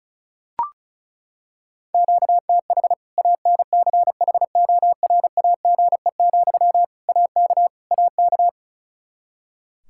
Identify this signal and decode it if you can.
Morse code, heard as a keyed tone, about 34 wpm, 700 Hz. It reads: QTH ANCHORAGE, AK AK
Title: Practice QSO - Exchange 1 - Beginning - 35wpm